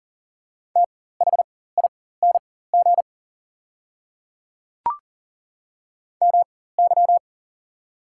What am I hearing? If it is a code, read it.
Morse code, heard as a keyed tone, about 40 wpm, spaced as for 19 wpm, 690 Hz. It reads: THING MY
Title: 4x Character Spacing - Top 100 Words - 40wpm